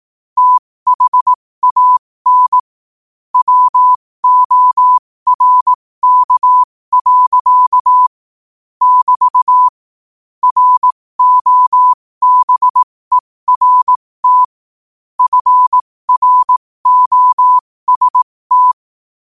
Morse code, heard as a keyed tone, about 18 wpm, spaced as for 15 wpm, 1000 Hz. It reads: THAN WORK. <BT> ROBERT FROST